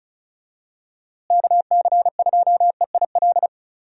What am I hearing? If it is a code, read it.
Morse code, heard as a keyed tone, about 36 wpm, 700 Hz. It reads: KC2EIL
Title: USA Call Signs - 6 Characters - Repeated 2x - 35wpm